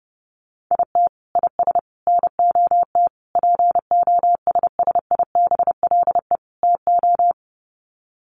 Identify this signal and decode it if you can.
Morse code, heard as a keyed tone, about 30 wpm, 700 Hz. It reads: IT IS NOT POSSIBLE TO
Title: Sets of 5 Words Encore2 30wpm